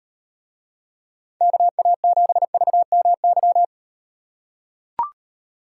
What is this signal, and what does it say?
Morse code, heard as a keyed tone, about 38 wpm, 700 Hz. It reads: KA7VMY